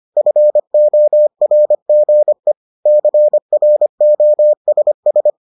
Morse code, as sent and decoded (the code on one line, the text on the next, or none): ..-. --- .-. --. / -.-. .-. --- ... ...
FORGE CROSS